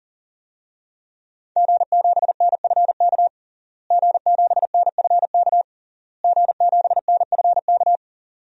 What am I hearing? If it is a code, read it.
Morse code, heard as a keyed tone, about 40 wpm, 700 Hz. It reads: G7NFK G7NFK G7NFK
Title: DXCC-100 Call Signs - 5 Characters - Repeated 3x - 40wpm